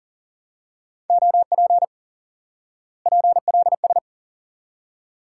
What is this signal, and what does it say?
Morse code, heard as a keyed tone, about 37 wpm, 700 Hz. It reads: OP PLS
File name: IQE Course 3 - Lesson 89 - YAGI - Two Elements - 40wpm